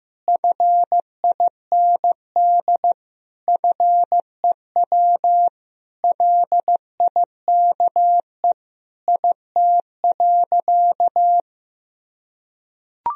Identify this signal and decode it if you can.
Morse code, heard as a keyed tone, about 15 wpm, 700 Hz. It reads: FIND FEW LIKE IT.
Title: Sentences from Top 1000 Words - Review Part 1 - 15wpm